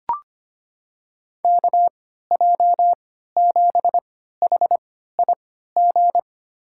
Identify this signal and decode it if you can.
Morse code, heard as a keyed tone, about 25 wpm, spaced as for 14 wpm, 700 Hz. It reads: KJ7HIG
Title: USA Call Signs - 6 Characters - 3x Character Spacing - 25wpm